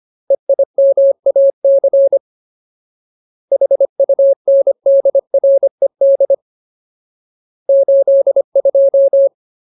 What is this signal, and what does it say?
Morse code, heard as a keyed tone, about 25 wpm, 550 Hz. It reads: EIMAC HUNDRED 82
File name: Ultimate Word List / 25WPM Ultimate